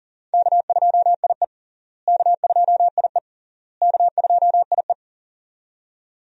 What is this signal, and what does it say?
Morse code, heard as a keyed tone, about 37 wpm, 700 Hz. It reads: K2IE K2IE K2IE
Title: USA Call Signs - 4 Characters - Repeated 3x - 40wpm